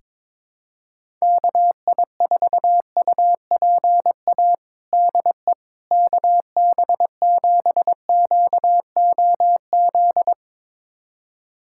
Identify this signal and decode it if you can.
Morse code, heard as a keyed tone, about 21 wpm, 700 Hz. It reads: KI4UPA DE KB7QOZ
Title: USA Callsign DE Callsign 22wpm